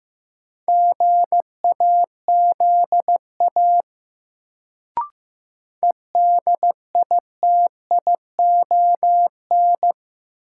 Morse code, heard as a keyed tone, about 15 wpm, 700 Hz. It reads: GAZA EDITION
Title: Sets of 5 Words - New Word Drill - 15wpm